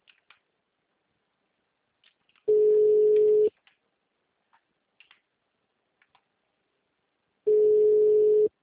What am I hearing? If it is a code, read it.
Ringback tone (Europe, 425 Hz).